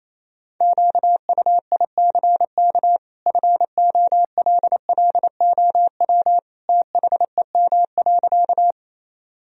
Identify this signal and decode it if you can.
Morse code, heard as a keyed tone, about 28 wpm, 700 Hz. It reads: QUICK FOLLOW THEM.
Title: Sentences from Top 1000 Words - Review Part 1 - 28wpm